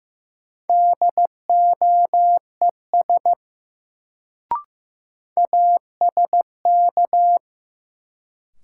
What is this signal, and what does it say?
Morse code, heard as a keyed tone, about 15 wpm, 700 Hz. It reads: DOES ASK
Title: Sentences from Top 200 Words - New Word Drill - 15wpm